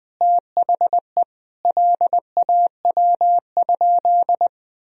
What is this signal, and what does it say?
Morse code, heard as a keyed tone, about 20 wpm, 700 Hz. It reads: THE LAW?
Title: Sentences from Top 1000 Words - Review Part 3 - 20wpm